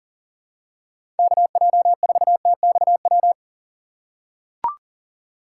Morse code, heard as a keyed tone, about 40 wpm, 690 Hz. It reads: KJ4TXW
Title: USA Call Signs - 6x Character Spacing - 40wpm